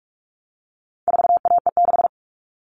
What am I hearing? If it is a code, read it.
Morse code, heard as a keyed tone, about 44 wpm, 700 Hz. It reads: 4AE6
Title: All Cut Numbers - 4-Digit 45wpm